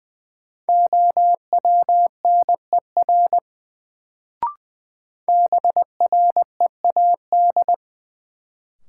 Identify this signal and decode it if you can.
Morse code, heard as a keyed tone, about 20 wpm, 700 Hz. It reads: OWNER BREAD
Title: Sentences from Top 1600 Words - New Word Drill - 20wpm